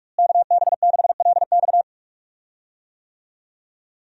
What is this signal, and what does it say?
Morse code, heard as a keyed tone, about 46 wpm, 700 Hz. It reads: KB6LX
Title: USA Call Signs - 5 Characters - Repeated 2x - 45wpm